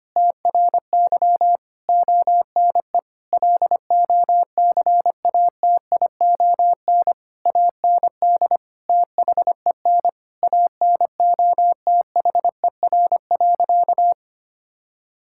Morse code, heard as a keyed tone, about 25 wpm, 700 Hz. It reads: TRY ONE LOCATION AND THEN ANOTHER.
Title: Sentences from Top 1000 Words - Review Part 6 - 25wpm